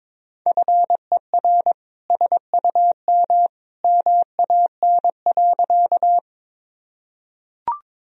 Morse code, heard as a keyed tone, about 22 wpm, 700 Hz. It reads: FER SUM MAN.
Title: Sentences from Top 200 Words and Abbreviations 22wpm